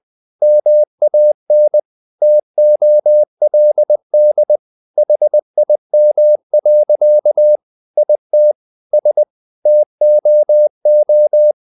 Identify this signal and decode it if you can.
Morse code, heard as a keyed tone, about 20 wpm, 600 Hz. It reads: MAN TOLD HIM. IT S TOO
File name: the old man and the sea / the old man and the sea part 1 20wpm